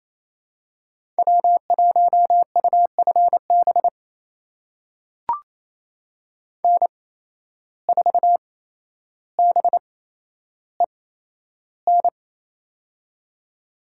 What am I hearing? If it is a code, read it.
Morse code, heard as a keyed tone, about 28 wpm, spaced as for 12 wpm, 700 Hz. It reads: W1UFB N4BEN
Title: USA Call Signs - 5 Characters - 8x Character Spacing - 28wpm